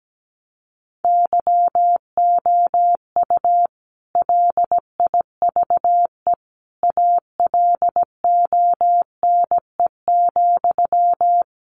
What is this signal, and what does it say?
Morse code, heard as a keyed tone, about 17 wpm, 700 Hz. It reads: YOU LIVE ALONE,
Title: Sentences from Top 1000 Words 17wpm